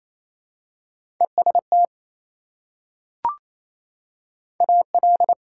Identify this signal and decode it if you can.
Morse code, heard as a keyed tone, about 28 wpm, 700 Hz. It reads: EST AL
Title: Prefixes and Suffixes 28wpm